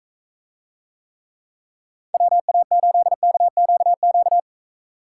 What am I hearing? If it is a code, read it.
Morse code, heard as a keyed tone, about 43 wpm, 700 Hz. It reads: WA8KQQ